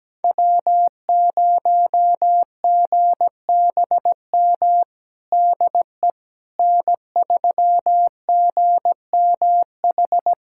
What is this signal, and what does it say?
Morse code, heard as a keyed tone, about 17 wpm, 700 Hz. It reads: W0GBM DE N3GMH
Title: USA Callsign DE Callsign - 5 Characters - 17wpm